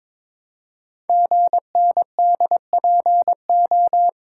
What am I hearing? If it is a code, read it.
Morse code, heard as a keyed tone, about 23 wpm, 700 Hz. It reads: GNDPO
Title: Code Groups - 5 Letters - 22wpm